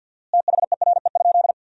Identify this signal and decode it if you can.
Morse code, heard as a keyed tone, about 47 wpm, 700 Hz. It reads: THERE?
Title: 4x Word-Spacing - Sentences from Top 100 Words - 50wpm